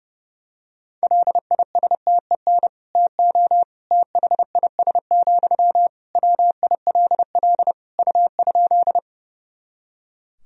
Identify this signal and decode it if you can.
Morse code, heard as a keyed tone, about 32 wpm, 700 Hz. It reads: LISTEN TO THIS, WILL U?
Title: Sentences from Top 400 Words and Abbreviations 30wpm